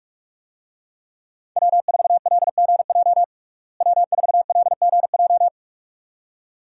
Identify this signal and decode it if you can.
Morse code, heard as a keyed tone, about 43 wpm, 700 Hz. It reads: W4LGJ W4LGJ
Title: USA Call Signs - 5 Characters - Repeated 2x - 45wpm